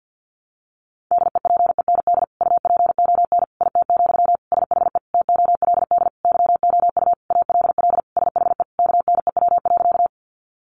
Morse code, heard as a keyed tone, about 50 wpm, 700 Hz. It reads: DEPEND UPON IT, SHE TOLD YOU ALL SHE KNEW.
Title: Sentences from Top 1300 Words 50wpm